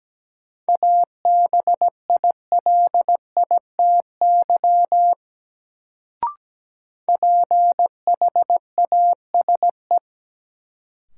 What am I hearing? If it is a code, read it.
Morse code, heard as a keyed tone, about 17 wpm, 700 Hz. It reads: ABILITY PHASE
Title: Sentences from Top 1200 Words - New Word Drill Alt - 17wpm